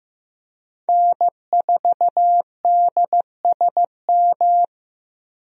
Morse code, heard as a keyed tone, about 15 wpm, 700 Hz. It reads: N4DSM
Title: USA Call Signs - 5 Characters - Repeated 3x - 15wpm